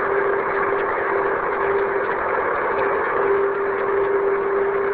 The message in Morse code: none